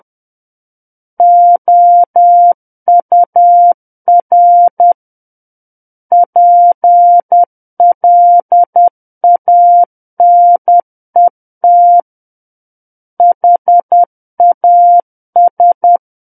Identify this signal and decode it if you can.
Morse code, heard as a keyed tone, about 10 wpm, 700 Hz. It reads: OUR PLANET HAS
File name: Book One / 01-Chapter 1 - 10 WPM